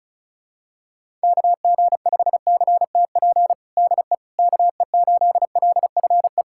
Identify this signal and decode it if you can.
Morse code, heard as a keyed tone, about 36 wpm, 700 Hz. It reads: KG5CTP DE KE8LFE
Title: USA Callsign DE Callsign - 6 Characters - 35wpm